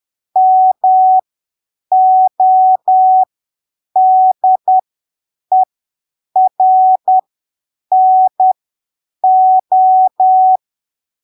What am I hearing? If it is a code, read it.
Morse code, heard as a keyed tone, about 10 wpm, spaced as for 7 wpm, 750 Hz. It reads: MODERNO
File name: arquivos / texto / texto 24 10ppm